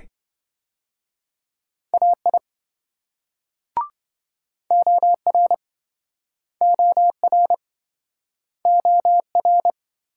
Morse code, heard as a keyed tone, about 26 wpm, 700 Hz. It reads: AI OR OR OR